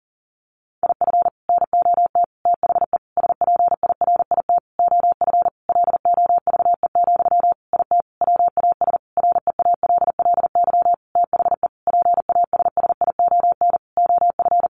Morse code, heard as a keyed tone, about 40 wpm, 700 Hz. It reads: IF NOT THE SPIRIT OF LOVE, IT WAS REALLY THE PASSION OF
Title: Sentences from Top 1400 Words 40wpm